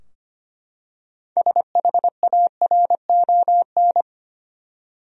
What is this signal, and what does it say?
Morse code, heard as a keyed tone, about 25 wpm, 700 Hz. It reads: SHARON
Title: USA Common Names - 25wpm